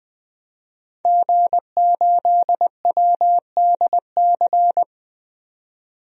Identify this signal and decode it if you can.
Morse code, heard as a keyed tone, about 20 wpm, 700 Hz. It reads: G8WDC
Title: DXCC-100 Call Signs - 5 Characters - 20wpm